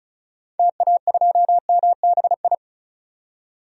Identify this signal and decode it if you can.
Morse code, heard as a keyed tone, about 36 wpm, 690 Hz. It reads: TA2MBI
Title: DXCC-100 Call Signs - 6 Characters - 35wpm